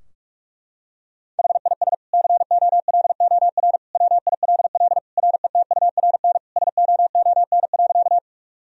Morse code, heard as a keyed tone, about 43 wpm, 700 Hz. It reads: HIS COLOR WILL RETURN SOON.
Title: Sentences from Top 1000 Words - Review Part 2 - 45wpm